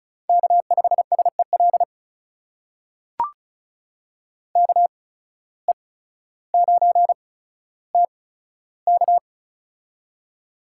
Morse code, heard as a keyed tone, about 34 wpm, spaced as for 15 wpm, 700 Hz. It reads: K5SEL KE9TK
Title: USA Call Signs - 5 Characters - 8x Character Spacing - 35wpm